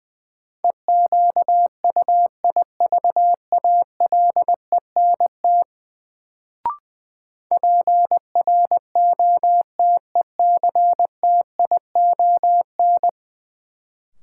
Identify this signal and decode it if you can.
Morse code, heard as a keyed tone, about 20 wpm, 700 Hz. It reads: EQUIVALENT PROTECTION
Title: Sentences from Top 1600 Words - New Word Drill Alt - 20wpm